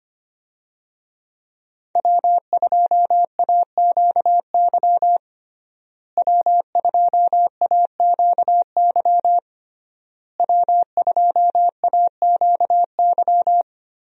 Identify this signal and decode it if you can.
Morse code, heard as a keyed tone, about 25 wpm, 700 Hz. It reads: W2AQY W2AQY W2AQY
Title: USA Call Signs - 5 Characters - Repeated 3x - 25wpm